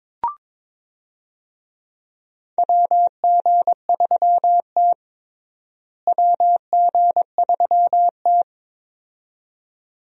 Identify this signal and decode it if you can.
Morse code, heard as a keyed tone, about 22 wpm, 700 Hz. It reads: WG3T WG3T
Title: USA Call Signs - 4 Characters - Repeated 2x - 22wpm